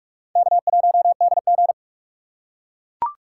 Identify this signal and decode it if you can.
Morse code, heard as a keyed tone, about 42 wpm, 700 Hz. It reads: K1DG